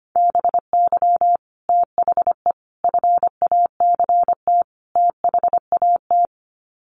Morse code, heard as a keyed tone, about 25 wpm, 700 Hz. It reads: BY THE FACT THAT
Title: Sets of 4 Words 25wpm